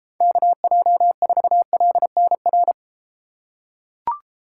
Morse code, heard as a keyed tone, about 32 wpm, 700 Hz. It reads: KJ4LNR